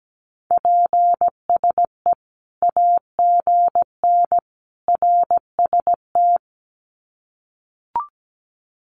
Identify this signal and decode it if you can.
Morse code, heard as a keyed tone, about 17 wpm, 700 Hz. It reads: PSE AGN RST
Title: Common QSO Phrases 17wpm